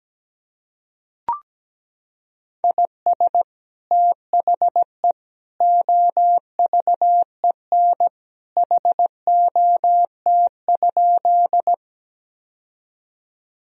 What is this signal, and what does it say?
Morse code, heard as a keyed tone, about 17 wpm, 700 Hz. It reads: IS THE OVEN HOT?
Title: Sentences from Top 1000 Words - Review Part 4 - 17wpm